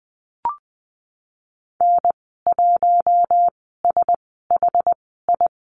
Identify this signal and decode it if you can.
Morse code, heard as a keyed tone, about 19 wpm, spaced as for 14 wpm, 700 Hz. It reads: N1SHI
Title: 076 Lesson 13g - Letter L - Call Signs 20wpm